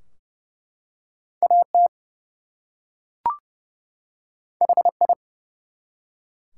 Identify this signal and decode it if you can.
Morse code, heard as a keyed tone, about 30 wpm, 700 Hz. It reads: AT HI
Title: Two Letter Words - 30wpm